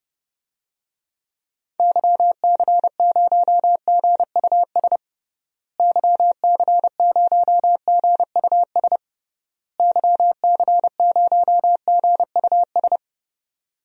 Morse code, heard as a keyed tone, about 30 wpm, 700 Hz. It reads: YC0GUS YC0GUS YC0GUS
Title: DXCC-100 Call Signs - 6 Characters - Repeated 3x - 30wpm